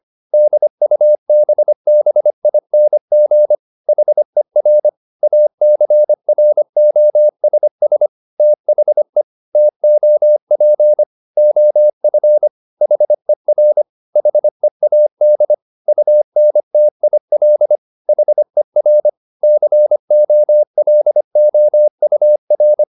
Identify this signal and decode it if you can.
Morse code, heard as a keyed tone, about 25 wpm, 600 Hz. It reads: DUBBING HER ACROSS THE TOP OF HER HEAD UNTIL HER COLOUR
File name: the old man and the sea part 1 25wpm